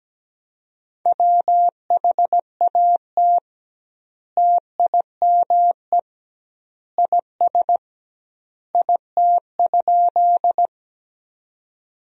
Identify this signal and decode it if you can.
Morse code, heard as a keyed tone, about 17 wpm, 700 Hz. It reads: WHAT TIME IS IT?
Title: 2x Word-Spacing - Sentences from Top 100 Words - 17wpm